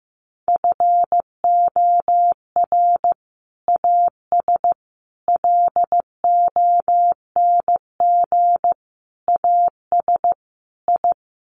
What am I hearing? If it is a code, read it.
Morse code, heard as a keyed tone, about 15 wpm, 700 Hz. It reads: FOR AS LONG AS I
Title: Sets of 5 Words Encore2 15wpm